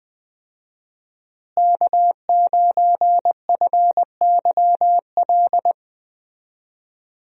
Morse code, heard as a keyed tone, about 20 wpm, 700 Hz. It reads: K9FYL
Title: USA Call Signs - 5 Characters - Repeated 2x - 20wpm